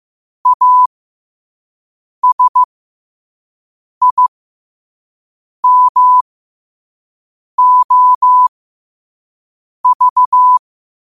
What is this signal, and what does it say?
Morse code, heard as a keyed tone, about 15 wpm, spaced as for 5 wpm, 1000 Hz. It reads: ASIMOV
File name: qotdfiles / 5wpm